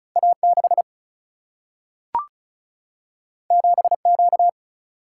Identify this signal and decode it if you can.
Morse code, heard as a keyed tone, about 36 wpm, 700 Hz. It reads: A 6 7 Q
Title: Single-Letter Single-Number 35wpm